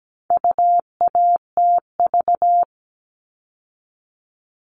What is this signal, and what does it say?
Morse code, heard as a keyed tone, about 17 wpm, 700 Hz. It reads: UATV
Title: All Cut Numbers - 4-Digit 17wpm